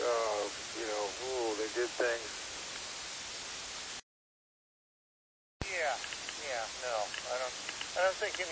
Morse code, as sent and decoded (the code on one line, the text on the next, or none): none